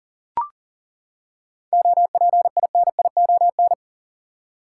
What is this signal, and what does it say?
Morse code, heard as a keyed tone, about 37 wpm, 700 Hz. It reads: OPINION
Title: Seven Letter Words - 40wpm